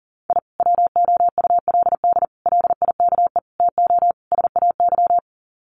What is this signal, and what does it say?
Morse code, heard as a keyed tone, about 40 wpm, 700 Hz. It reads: I WOULD LIKE TO SAY
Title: Sets of 5 Words Encore2 40wpm